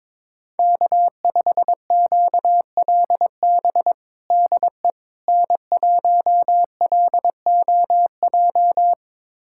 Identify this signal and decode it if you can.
Morse code, heard as a keyed tone, about 22 wpm, 700 Hz. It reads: K5QLB DE N1LOJ
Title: USA Callsign DE Callsign - 5 Characters - 22wpm